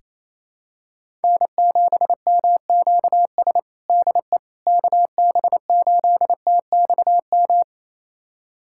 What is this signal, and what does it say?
Morse code, heard as a keyed tone, about 28 wpm, 700 Hz. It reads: N7MQS DE KB8TXM
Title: USA Callsign DE Callsign 28wpm